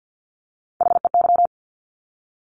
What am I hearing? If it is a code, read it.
Morse code, heard as a keyed tone, about 50 wpm, 700 Hz. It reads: HEY
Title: Top Three Letter Combinations in Words 50wpm